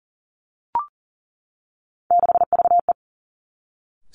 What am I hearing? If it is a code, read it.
Morse code, heard as a keyed tone, about 39 wpm, 700 Hz. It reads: 6VE
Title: Three Challenging Letter-Number Combinations 40wpm